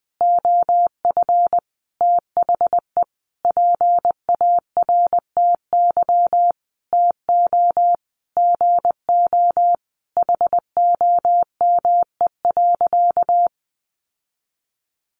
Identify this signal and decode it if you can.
Morse code, heard as a keyed tone, about 20 wpm, 700 Hz. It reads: OF THE PARTY TO GO HOME.
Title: Sentences from Top 1400 Words 20wpm